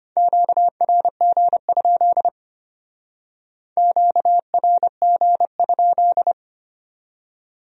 Morse code, heard as a keyed tone, about 32 wpm, 700 Hz. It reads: QRG? QRG?